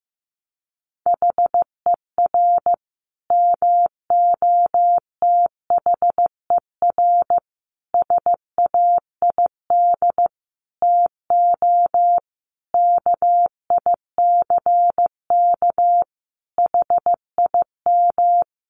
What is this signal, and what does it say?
Morse code, heard as a keyed tone, about 15 wpm, 700 Hz. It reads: HER MOTHER SAID TO KICK HIM
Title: Sentences from Top 1300 Words 15wpm